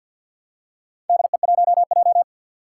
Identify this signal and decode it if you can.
Morse code, heard as a keyed tone, about 48 wpm, 700 Hz. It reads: DE1J
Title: Four Challenging Letter-Number Combinations 50wpm